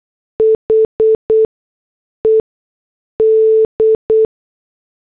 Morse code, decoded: HED